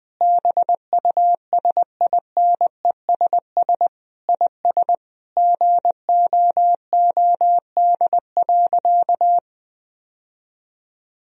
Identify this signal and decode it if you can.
Morse code, heard as a keyed tone, about 20 wpm, 700 Hz. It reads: BUSINESS IS GOOD.
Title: Sentences from Top 1000 Words - Review Part 4 - 20wpm